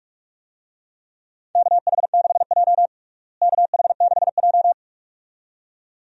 Morse code, heard as a keyed tone, about 41 wpm, 700 Hz. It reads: KH6J KH6J
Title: DXCC-100 Call Signs - 4 Characters - Repeated 2x - 45wpm